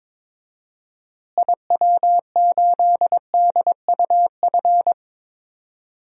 Morse code, heard as a keyed tone, about 22 wpm, 700 Hz. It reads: IW8DUF
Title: DXCC-100 Call Signs - 6 Characters - Repeated 2x - 22wpm